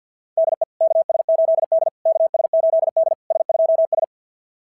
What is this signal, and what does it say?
Morse code, heard as a keyed tone, about 46 wpm, 660 Hz. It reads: DE KS8D KS8D S2S
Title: SOTA QSO Practice 50wpm